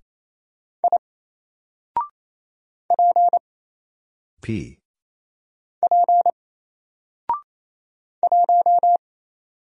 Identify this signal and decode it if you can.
Morse code, heard as a keyed tone, about 28 wpm, 700 Hz. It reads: I P P 1
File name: Single-Letter-Number-28wpm